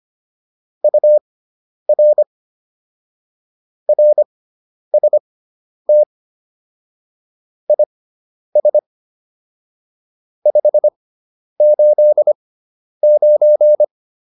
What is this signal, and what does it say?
Morse code, heard as a keyed tone, about 25 wpm, spaced as for 10 wpm, 600 Hz. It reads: UR RST IS 589
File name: QSO302 10